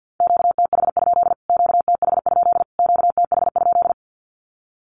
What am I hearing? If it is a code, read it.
Morse code, heard as a keyed tone, about 50 wpm, 700 Hz. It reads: QTH? QTH? QTH?
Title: Q-Codes Repeated 3x 50wpm